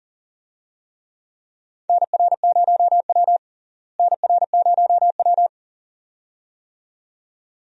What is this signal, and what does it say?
Morse code, heard as a keyed tone, about 40 wpm, 700 Hz. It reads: NR0W NR0W
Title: USA Call Signs - 4 Characters - Repeated 2x - 40wpm